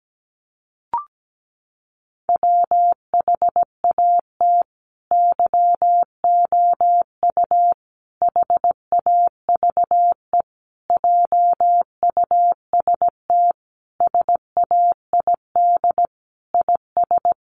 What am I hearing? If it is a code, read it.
Morse code, heard as a keyed tone, about 17 wpm, 700 Hz. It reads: WHAT YOU HAVE JUST SAID IS
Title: Sentences from Top 200 Words 17wpm